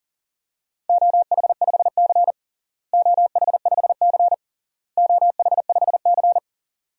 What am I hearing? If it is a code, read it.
Morse code, heard as a keyed tone, about 37 wpm, 700 Hz. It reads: OH5C OH5C OH5C
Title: DXCC-100 Call Signs - 4 Characters - Repeated 3x - 40wpm